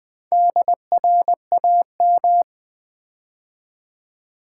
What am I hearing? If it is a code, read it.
Morse code, heard as a keyed tone, about 20 wpm, 700 Hz. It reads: DRAM
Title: Build A Word - 4 Letters At A Time - 12-Letter Words - 20wpm